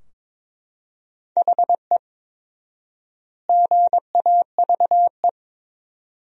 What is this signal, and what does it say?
Morse code, heard as a keyed tone, about 22 wpm, 700 Hz. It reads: HE GAVE
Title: 4x Word-Spacing - Sentences from Top 500 Words - 22wpm